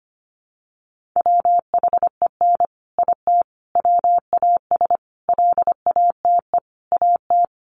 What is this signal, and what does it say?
Morse code, heard as a keyed tone, about 25 wpm, 700 Hz. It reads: WHEN IT WAS LATE AT
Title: Sentences from Top 600 Words 25wpm